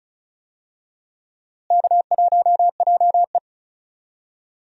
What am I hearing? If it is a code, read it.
Morse code, heard as a keyed tone, about 36 wpm, 700 Hz. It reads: K1JE
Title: USA Call Signs - 4 Characters - Repeated 3x - 35wpm